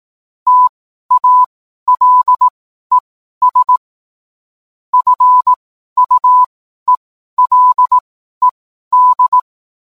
Morse code, heard as a keyed tone, about 18 wpm, spaced as for 12 wpm, 1000 Hz. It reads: TALES FUELED